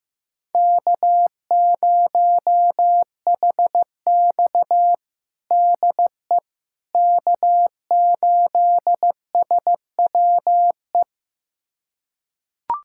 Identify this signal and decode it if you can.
Morse code, heard as a keyed tone, about 15 wpm, 700 Hz. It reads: K0HX DE K8SWE
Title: USA Callsign DE Callsign 15wpm